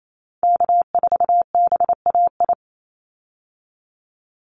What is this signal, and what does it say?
Morse code, heard as a keyed tone, about 28 wpm, 700 Hz. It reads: K4BAI
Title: Call Signs Repeated 3x - 28wpm